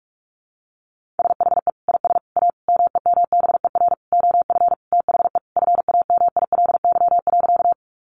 Morse code, heard as a keyed tone, about 44 wpm, 700 Hz. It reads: SHE IS A MEMBER OF THE FAMILY.
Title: Sentences from Top 1100 Words 45wpm